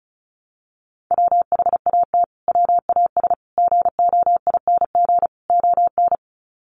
Morse code, heard as a keyed tone, about 34 wpm, 700 Hz. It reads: WHAT WAS GOING ON
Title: Sets of 4 Words 35wpm